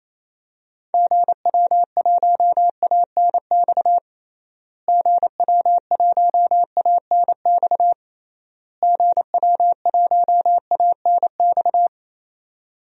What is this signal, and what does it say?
Morse code, heard as a keyed tone, about 28 wpm, 700 Hz. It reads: GW1ANX GW1ANX GW1ANX
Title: DXCC-100 Call Signs - 6 Characters - Repeated 3x - 28wpm